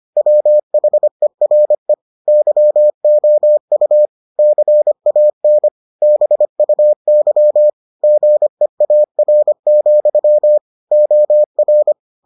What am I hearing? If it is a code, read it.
Morse code, heard as a keyed tone, about 25 wpm, 600 Hz. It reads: WHERE YOU CAN BUY GEAR, OR